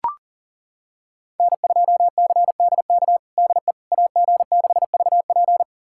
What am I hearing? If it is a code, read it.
Morse code, heard as a keyed tone, about 40 wpm, 690 Hz. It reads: N2CDK DE AG6VP
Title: USA Callsign DE Callsign - 5 Characters - 40wpm